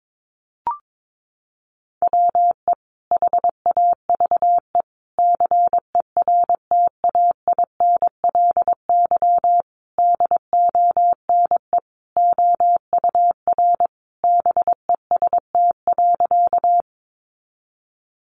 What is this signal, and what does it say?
Morse code, heard as a keyed tone, about 22 wpm, 700 Hz. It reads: WE HAVE CERTAINLY DONE OUR BEST.
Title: Sentences from Top 900 Words 22wpm